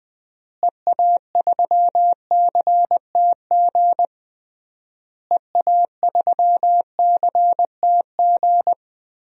Morse code, heard as a keyed tone, about 20 wpm, 700 Hz. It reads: EA3CTG EA3CTG
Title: DXCC-100 Call Signs - 6 Characters - Repeated 3x - 20wpm